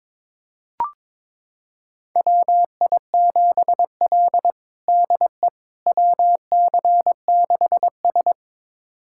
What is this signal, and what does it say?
Morse code, heard as a keyed tone, about 22 wpm, 700 Hz. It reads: WI7L DE WC6S
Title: USA Callsign DE Callsign - 4 Characters - 22wpm